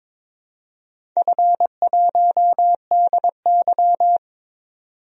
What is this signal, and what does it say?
Morse code, heard as a keyed tone, about 22 wpm, 700 Hz. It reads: F1DY
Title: DXCC-100 Call Signs - 4 Characters - 22wpm